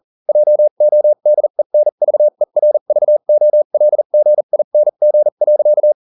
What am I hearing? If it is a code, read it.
Morse code, heard as a keyed tone, about 40 wpm, spaced as for 35 wpm, 600 Hz. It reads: JODENVERVOLGING.